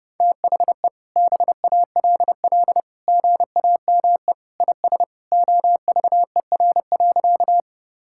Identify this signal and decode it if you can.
Morse code, heard as a keyed tone, about 30 wpm, 700 Hz. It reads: THE BALL GAME IS OVER.
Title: Sentences from Top 1000 Words - Review Part 6 - 30wpm